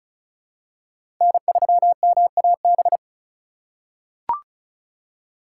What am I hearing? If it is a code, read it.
Morse code, heard as a keyed tone, about 34 wpm, 700 Hz. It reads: N3MAB
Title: USA Call Signs - 5 Characters - 4x Character Spacing - 35wpm